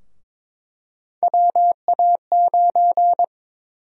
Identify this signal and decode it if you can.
Morse code, heard as a keyed tone, about 23 wpm, 700 Hz. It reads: WA9
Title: USA Call Sign Prefixes - Repeated 2x - 22wpm